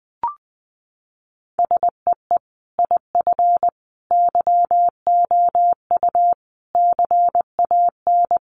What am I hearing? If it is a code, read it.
Morse code, heard as a keyed tone, about 20 wpm, 700 Hz. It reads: SEE IF YOU CAN
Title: Sentences from Top 1000 Words 20wpm